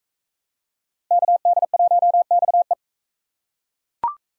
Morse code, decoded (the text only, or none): KD1XE